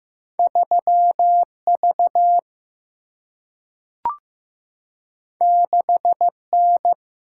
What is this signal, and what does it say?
Morse code, heard as a keyed tone, about 15 wpm, 700 Hz. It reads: 3 V 6 N
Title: Single-Letter Single-Number 15wpm